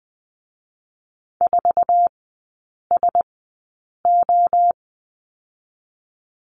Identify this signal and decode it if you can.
Morse code, heard as a keyed tone, about 19 wpm, spaced as for 8 wpm, 700 Hz. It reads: 4SO
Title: 060 Lesson 11e - Letter h - Three Characters 20wpm